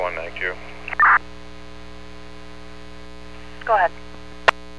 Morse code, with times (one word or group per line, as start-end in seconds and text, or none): none